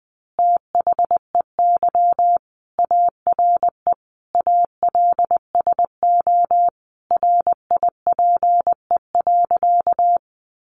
Morse code, heard as a keyed tone, about 20 wpm, 700 Hz. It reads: THEY ARE ALSO RIPE.
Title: Additional Easy Phrases 20wpm